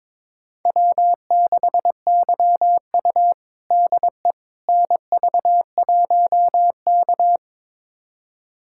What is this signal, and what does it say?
Morse code, heard as a keyed tone, about 22 wpm, 700 Hz. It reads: W6YU DE NV1K
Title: USA Callsign DE Callsign - 4 Characters - 22wpm